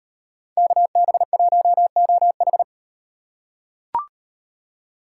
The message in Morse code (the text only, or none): KB1OH